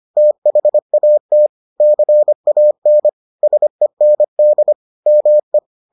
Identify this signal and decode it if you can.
Morse code, heard as a keyed tone, about 25 wpm, 600 Hz. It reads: THAT CAN SEND ME